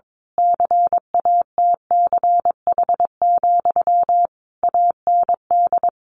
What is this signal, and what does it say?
Morse code, heard as a keyed tone, about 22 wpm, 700 Hz. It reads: CATCH, AND
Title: A Sound Of Thunder 22wpm